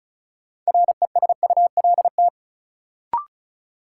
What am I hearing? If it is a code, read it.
Morse code, heard as a keyed tone, about 36 wpm, 700 Hz. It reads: RESULT
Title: Sets of 3 Words - New Word Drill - 35wpm